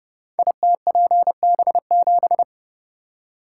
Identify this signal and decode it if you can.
Morse code, heard as a keyed tone, about 30 wpm, 700 Hz. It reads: ITPB7
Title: Code Groups - 5 Letter:Numbers - 30wpm